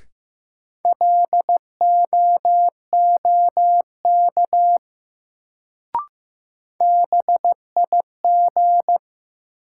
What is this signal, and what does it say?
Morse code, heard as a keyed tone, about 16 wpm, 700 Hz. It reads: LOOK BIG
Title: Sets of 2 Words - New Word Drill - 15wpm